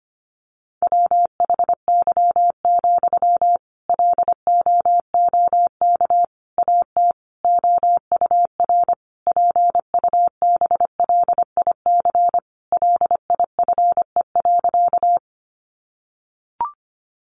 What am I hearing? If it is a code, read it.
Morse code, heard as a keyed tone, about 25 wpm, 700 Hz. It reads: WHY, LOOK AT OUR PUBLIC LIFE.
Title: Sentences from Top 600 Words 25wpm